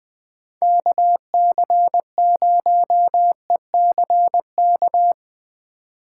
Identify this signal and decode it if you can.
Morse code, heard as a keyed tone, about 23 wpm, 700 Hz. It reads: KC0ECK
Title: Speed Racing - USA 2x3 Call Signs 20wpm